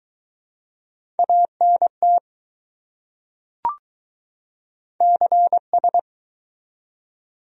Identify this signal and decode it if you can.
Morse code, heard as a keyed tone, about 20 wpm, 700 Hz. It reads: ANT CS